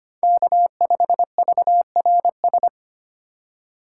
Morse code, heard as a keyed tone, about 25 wpm, 700 Hz. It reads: K5VRS
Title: USA Call Signs - 5 Characters - 3x Character Spacing - 25wpm